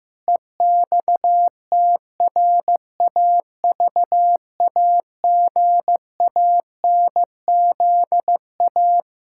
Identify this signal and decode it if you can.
Morse code, heard as a keyed tone, about 15 wpm, 700 Hz. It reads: EXTRAVAGANZA
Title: Pangram Challenge v2 - 15wpm